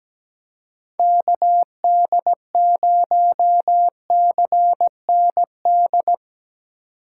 Morse code, heard as a keyed tone, about 17 wpm, 700 Hz. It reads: KD0CND
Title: USA Call Signs - 6 Characters - 2x Character Spacing - 17wpm